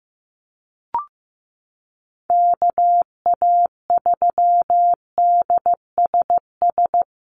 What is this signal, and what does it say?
Morse code, heard as a keyed tone, about 15 wpm, 700 Hz. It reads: KA3DSS
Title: USA Call Signs Repeated 3x 15wpm